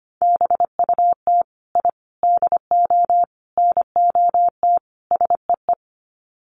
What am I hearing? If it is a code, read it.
Morse code, heard as a keyed tone, about 25 wpm, 700 Hz. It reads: BUT I DO NOT SEE
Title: Sets of 5 Words Encore 25wpm